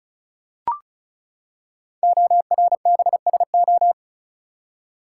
Morse code, heard as a keyed tone, about 36 wpm, 700 Hz. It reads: ORBSO
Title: Code Groups - 5 Letter:Numbers - 35wpm